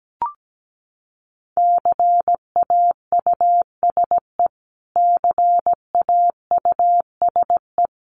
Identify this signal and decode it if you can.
Morse code, heard as a keyed tone, about 17 wpm, 700 Hz. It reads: CAUSE CAUSE
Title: Top 200 Words Repeated 17wpm